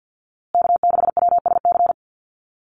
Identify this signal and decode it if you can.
Morse code, heard as a keyed tone, about 50 wpm, 700 Hz. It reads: K6WSC
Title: USA Call Signs Repeated 2x 50wpm